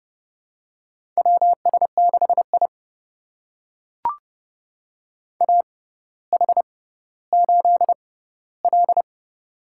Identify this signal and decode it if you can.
Morse code, heard as a keyed tone, about 30 wpm, spaced as for 15 wpm, 700 Hz. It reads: WS6I AH8L
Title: USA Call Signs - 4 Characters - 6x Character Spacing - 30wpm